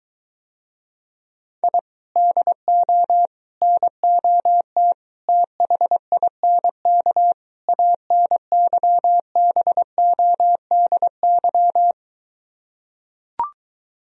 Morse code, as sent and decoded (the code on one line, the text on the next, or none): .. / -.. --- / -. --- - / - .... .. -. -.- / .- -. -.-- -... --- -.. -.--
I DO NOT THINK ANYBODY